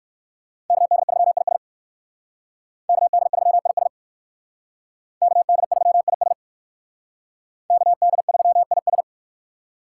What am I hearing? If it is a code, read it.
Morse code, heard as a keyed tone, about 56 wpm, 700 Hz. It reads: KD3IS KD3IS KD3IS KD3IS
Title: Speed Racing - USA 2x2 Call Signs 45wpm